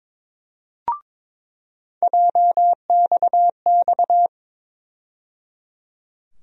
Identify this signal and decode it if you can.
Morse code, heard as a keyed tone, about 21 wpm, 700 Hz. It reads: JXX
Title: USA Call Sign Postfixes - 3 Letters 22wpm